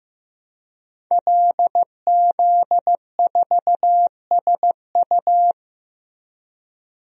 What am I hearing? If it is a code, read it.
Morse code, heard as a keyed tone, about 15 wpm, 700 Hz. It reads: LZ4SU
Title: DXCC-100 Call Signs - 5 Characters - 15wpm